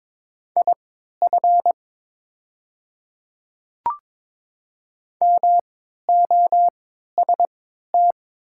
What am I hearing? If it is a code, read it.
Morse code, heard as a keyed tone, about 22 wpm, spaced as for 12 wpm, 700 Hz. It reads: IF MOST
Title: 3x Character Spacing - Top 100 Words - 22wpm